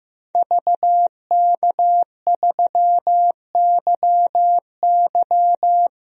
Morse code, heard as a keyed tone, about 15 wpm, 700 Hz. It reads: VK3YY
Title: DXCC-100 Call Signs - 5 Characters - 15wpm